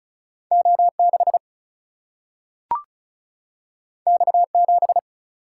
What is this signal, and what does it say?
Morse code, heard as a keyed tone, about 36 wpm, 700 Hz. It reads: O 6 X 7
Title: Single-Letter Single-Number 35wpm